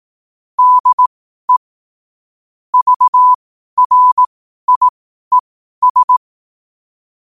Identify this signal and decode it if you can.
Morse code, heard as a keyed tone, about 18 wpm, spaced as for 12 wpm, 1000 Hz. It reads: DE VRIES
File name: qotdfiles / 12wpm